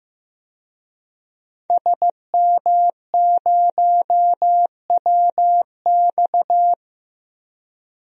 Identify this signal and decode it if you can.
Morse code, heard as a keyed tone, about 15 wpm, 700 Hz. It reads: SM0WX
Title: DXCC-100 Call Signs - 5 Characters - Repeated 2x - 15wpm